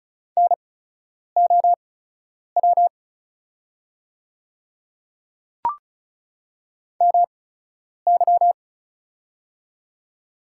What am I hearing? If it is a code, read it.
Morse code, heard as a keyed tone, about 34 wpm, spaced as for 10 wpm, 690 Hz. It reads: NOW MY
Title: 8x Character Spacing - Top 100 Words - 35wpm